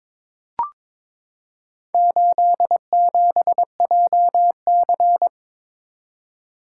Morse code, heard as a keyed tone, about 22 wpm, 700 Hz. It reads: 87JC
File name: practice / 22wpm / Four Letter-Number Combinations - 22wpm